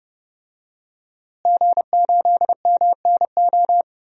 Morse code, heard as a keyed tone, about 30 wpm, 700 Hz. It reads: G8MNO
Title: DXCC-100 Call Signs - 5 Characters - Repeated 2x - 30wpm